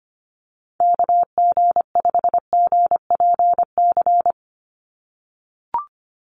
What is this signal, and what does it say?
Morse code, heard as a keyed tone, about 25 wpm, 700 Hz. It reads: KG5GPC
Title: USA Call Signs 25wpm